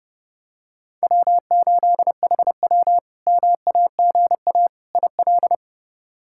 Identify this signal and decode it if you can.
Morse code, heard as a keyed tone, about 32 wpm, 700 Hz. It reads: W8HW MAGA IL
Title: K1USN SST QSO Practice - 30wpm